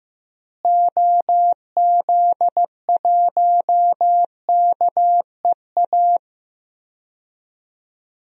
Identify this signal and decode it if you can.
Morse code, heard as a keyed tone, about 16 wpm, 700 Hz. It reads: OZ1KEA
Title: DXCC-100 Call Signs - 6 Characters - Repeated 2x - 15wpm